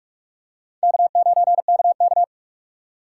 Morse code, heard as a keyed tone, about 48 wpm, 700 Hz. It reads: K9KK